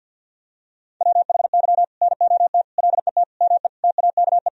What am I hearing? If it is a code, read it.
Morse code, heard as a keyed tone, about 46 wpm, 700 Hz. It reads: WHY NOT LET ME TAKE
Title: Sentences from Top 1000 Words - Review Part 2 - 50wpm